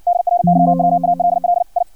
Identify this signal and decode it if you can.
Morse code, heard as a keyed tone, about 56 wpm, 700 Hz. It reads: DRYWALL I
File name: archived / CRSnet